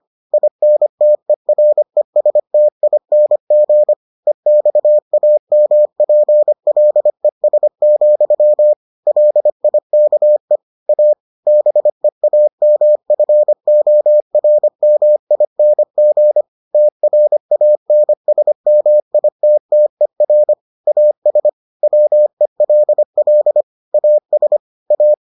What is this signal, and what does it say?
Morse code, heard as a keyed tone, about 25 wpm, 600 Hz. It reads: INTERESTING EXAMPLES, LIKE A BEAMFORMING TRANSMITTER AS WELL AS A